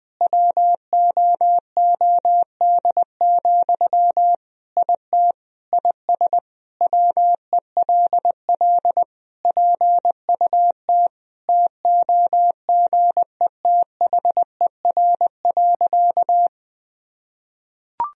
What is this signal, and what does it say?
Morse code, heard as a keyed tone, about 20 wpm, 700 Hz. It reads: WOOD, IT IS WELL PUT TOGETHER.
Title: Sentences from Top 1000 Words - Review Part 3 - 20wpm